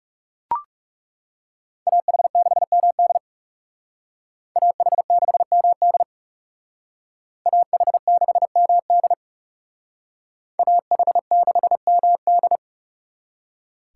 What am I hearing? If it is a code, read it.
Morse code, heard as a keyed tone, about 36 wpm, 700 Hz. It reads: AH6MD AH6MD AH6MD AH6MD
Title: Speed Racing - USA 2x2 Call Signs 30wpm